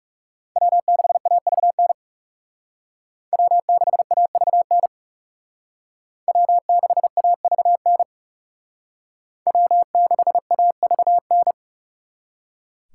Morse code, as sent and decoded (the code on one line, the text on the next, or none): .-- -.... .- ...- -. / .-- -.... .- ...- -. / .-- -.... .- ...- -. / .-- -.... .- ...- -.
W6AVN W6AVN W6AVN W6AVN